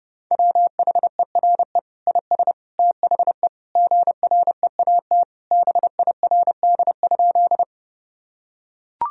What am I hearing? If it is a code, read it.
Morse code, heard as a keyed tone, about 32 wpm, 700 Hz. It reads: WHERE IS THE GREAT BIRD?
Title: Sentences from Top 1000 Words - Review Part 2 - 30wpm